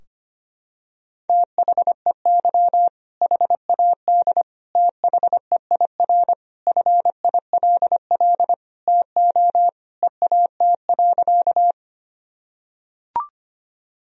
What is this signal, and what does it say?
Morse code, heard as a keyed tone, about 25 wpm, 700 Hz. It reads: THEY HAD THEIR FILL TO EAT.
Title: Sentences from Top 1000 Words - Review Part 3 - 25wpm